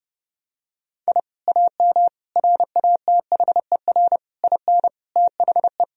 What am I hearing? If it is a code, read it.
Morse code, heard as a keyed tone, about 30 wpm, 700 Hz. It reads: I AM RATHER IN THE
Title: Sentences from Top 1000 Words - Review Part 4 - 30wpm